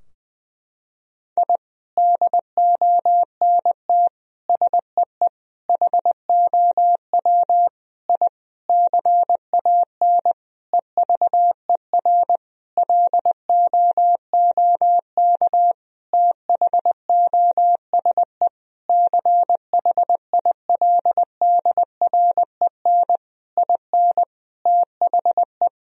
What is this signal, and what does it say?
Morse code, heard as a keyed tone, about 20 wpm, 700 Hz. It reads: I DONT SEE HOW I CAN EVER LOOK THOSE CHILDREN IN THE
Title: Sentences from Top 1000 Words - Review Part 2 - 20wpm